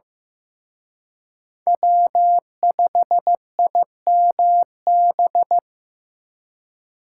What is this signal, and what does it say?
Morse code, heard as a keyed tone, about 18 wpm, 700 Hz. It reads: W5IMB
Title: Speed Racing - USA 1x3 Call Signs 15wpm